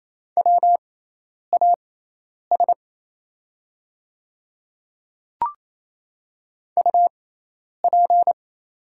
Morse code, heard as a keyed tone, about 28 wpm, spaced as for 10 wpm, 700 Hz. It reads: WAS UP
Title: 6x Character Spacing - Top 100 Words - 28wpm